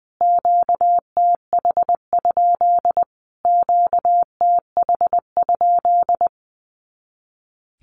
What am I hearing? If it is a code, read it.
Morse code, heard as a keyed tone, about 20 wpm, 700 Hz. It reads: QTH? QTH?
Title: Q-Codes Repeated 3x 20wpm